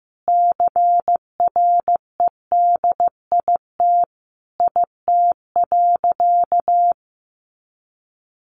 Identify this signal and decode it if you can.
Morse code, heard as a keyed tone, about 15 wpm, 700 Hz. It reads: CREDIT IT.
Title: Sentences from Top 1000 Words 15wpm